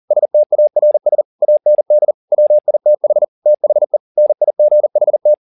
Morse code, heard as a keyed tone, about 40 wpm, 600 Hz. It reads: STARS AND WITH THE NIGHT
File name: the old man and the sea 2 40wpm